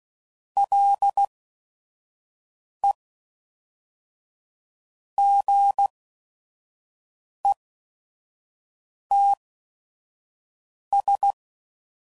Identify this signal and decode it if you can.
Morse code, heard as a keyed tone, about 16 wpm, spaced as for 5 wpm, 800 Hz. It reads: LE GETS
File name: CW Code Groups @5WPM / Technical Subjects / AND Gate